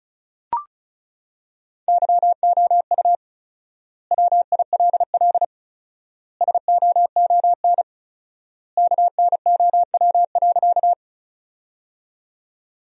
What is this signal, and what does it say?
Morse code, heard as a keyed tone, about 36 wpm, 700 Hz. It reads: YOU WILL SOON KNOW.
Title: 4x Word-Spacing - Sentences from Top 400 Words - 35wpm